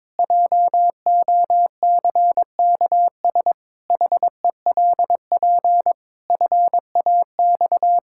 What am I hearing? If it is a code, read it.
Morse code, heard as a keyed tone, about 22 wpm, 700 Hz. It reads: JOCKS HELP FAX
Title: Pangram Challenge v2 - 22wpm